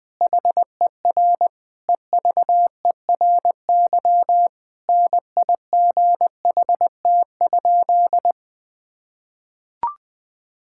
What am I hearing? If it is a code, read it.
Morse code, heard as a keyed tone, about 20 wpm, 700 Hz. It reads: HER EVERY NIGHT?
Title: Sentences from Top 1000 Words - Review Part 5 - 20wpm